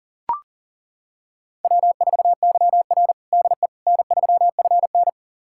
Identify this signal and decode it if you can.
Morse code, heard as a keyed tone, about 40 wpm, 700 Hz. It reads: W4YR DE N3FN
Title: USA Callsign DE Callsign - 4 Characters - 40wpm